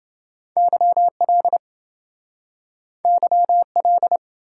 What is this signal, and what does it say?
Morse code, heard as a keyed tone, about 27 wpm, 700 Hz. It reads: YL YL